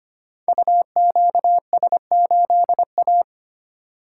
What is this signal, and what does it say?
Morse code, heard as a keyed tone, about 25 wpm, 700 Hz. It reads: UQS8A
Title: Code Groups - 5 Letter:Numbers - 25wpm